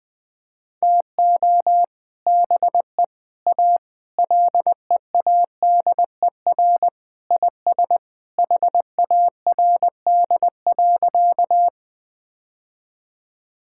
Morse code, heard as a keyed tone, about 21 wpm, 700 Hz. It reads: TO BE A LEADER IS HARD.
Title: Sentences with 3-Word Phrases - Spoken Beginning - 20wpm